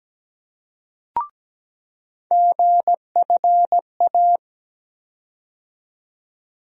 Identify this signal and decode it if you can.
Morse code, heard as a keyed tone, about 17 wpm, 700 Hz. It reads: GFA
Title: USA Call Sign Postfixes - 3 Letters 17wpm